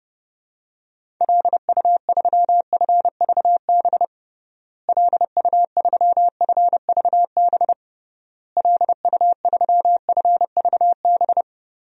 Morse code, heard as a keyed tone, about 29 wpm, 700 Hz. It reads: LU3FVB LU3FVB LU3FVB
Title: DXCC-100 Call Signs - 6 Characters - Repeated 3x - 30wpm